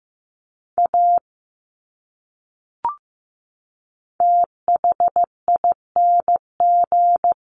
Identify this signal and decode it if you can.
Morse code, heard as a keyed tone, about 15 wpm, 700 Hz. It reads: A THING
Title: Top 200 Words 15wpm